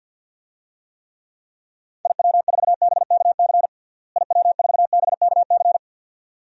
Morse code, heard as a keyed tone, about 52 wpm, 700 Hz. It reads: IW4BKC IW4BKC
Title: DXCC-100 Call Signs - 6 Characters - Repeated 2x - 50wpm